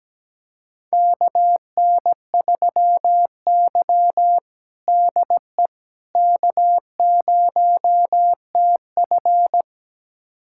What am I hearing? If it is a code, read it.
Morse code, heard as a keyed tone, about 17 wpm, 700 Hz. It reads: KN3Y DE K0TF
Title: USA Callsign DE Callsign - 4 Characters - 17wpm